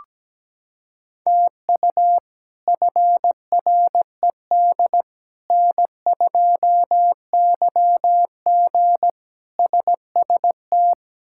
Morse code, heard as a keyed tone, about 17 wpm, 700 Hz. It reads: TU FRED N2YG SST
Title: K1USN SST QSO Practice - 17wpm